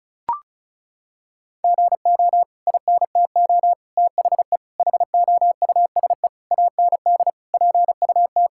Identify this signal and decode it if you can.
Morse code, heard as a keyed tone, about 36 wpm, 700 Hz. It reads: GO INTO THE HOUSE AND PUT
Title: Sentences from Top 1000 Words - Review Part 5 - 35wpm